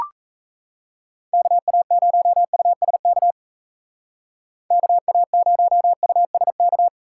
Morse code, heard as a keyed tone, about 38 wpm, 700 Hz. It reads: KA0USK KA0USK